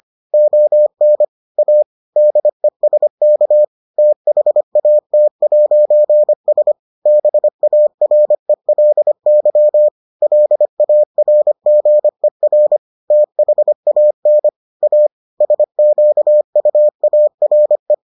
Morse code, heard as a keyed tone, about 25 wpm, 600 Hz. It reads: ON A DESK THAT'S BARELY LARGER THAN A SQUARE